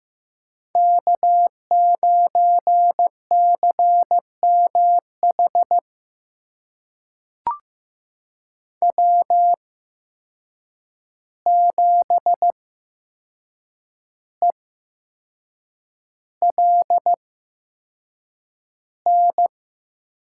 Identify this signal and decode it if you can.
Morse code, heard as a keyed tone, about 15 wpm, spaced as for 6 wpm, 700 Hz. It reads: K9CMH W7ELN
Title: USA Call Signs - 5 Characters - 8x Character Spacing - 15wpm